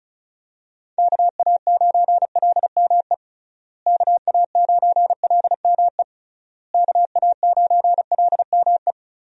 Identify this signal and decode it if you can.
Morse code, heard as a keyed tone, about 34 wpm, 700 Hz. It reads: KA9LME KA9LME KA9LME
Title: USA Call Signs - 6 Characters - Repeated 3x - 35wpm